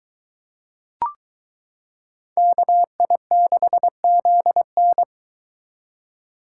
Morse code, heard as a keyed tone, about 23 wpm, 700 Hz. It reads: KI6ZN